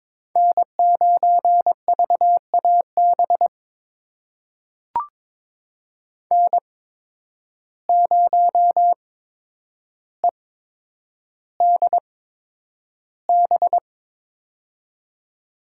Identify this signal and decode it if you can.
Morse code, heard as a keyed tone, about 22 wpm, spaced as for 9 wpm, 700 Hz. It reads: N9VAB N0EDB
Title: USA Call Signs - 5 Characters - 8x Character Spacing - 22wpm